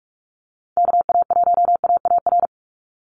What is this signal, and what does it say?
Morse code, heard as a keyed tone, about 44 wpm, 700 Hz. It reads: KA1AAR
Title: USA Call Signs 45wpm